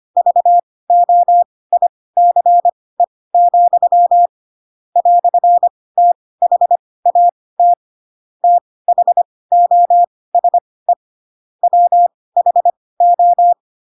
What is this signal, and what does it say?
Morse code, heard as a keyed tone, about 25 wpm, spaced as for 18 wpm, 700 Hz. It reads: VOICE, "THAT THOSE WHO
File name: Pinocchio / Pinocchio ch4 12 25 ik1hge